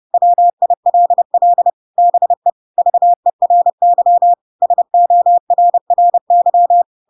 Morse code, heard as a keyed tone, about 30 wpm, 700 Hz. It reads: WILL BE VERY SORRY
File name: Pinocchio / Pinocchio ch4 30 30 ik1hge